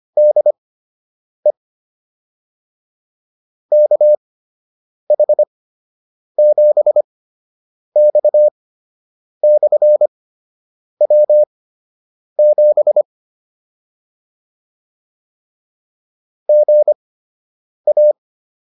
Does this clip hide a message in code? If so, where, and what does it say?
Yes: Morse code from 0.2 to 18.1 s, heard as a keyed tone, about 25 wpm, spaced as for 8 wpm, 600 Hz. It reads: DE KH7X/W7 GA